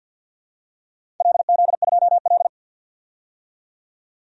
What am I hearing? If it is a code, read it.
Morse code, heard as a keyed tone, about 51 wpm, 700 Hz. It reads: LZ2L
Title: DXCC-100 Call Signs - 4 Characters - 50wpm